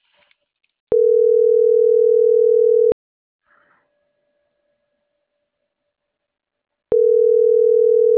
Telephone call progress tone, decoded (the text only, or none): ringback tone